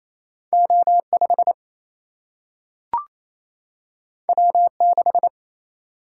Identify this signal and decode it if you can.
Morse code, heard as a keyed tone, about 28 wpm, 700 Hz. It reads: O5 W6
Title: Single-Letter Single-Number 28wpm